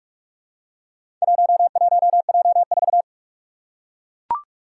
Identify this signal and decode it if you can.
Morse code, heard as a keyed tone, about 41 wpm, 700 Hz. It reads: 11J4
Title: Four Challenging Letter-Number Combinations 45wpm